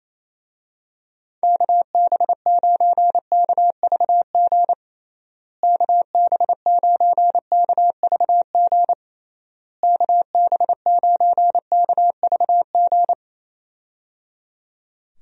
Morse code, heard as a keyed tone, about 28 wpm, 700 Hz. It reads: KB9KVG KB9KVG KB9KVG
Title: USA Call Signs - 6 Characters - Repeated 3x - 28wpm